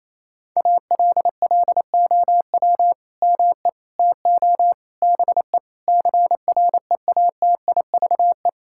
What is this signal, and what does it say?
Morse code, heard as a keyed tone, about 28 wpm, 700 Hz. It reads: ALLOW ME TO BE CREATIVE
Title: Sentences from Top 1000 Words - Review Part 5 - 28wpm